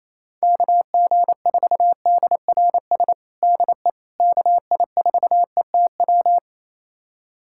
Morse code, heard as a keyed tone, about 28 wpm, 700 Hz. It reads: KG4DRS DE KI4ETW
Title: USA Callsign DE Callsign - 6 Characters - 28wpm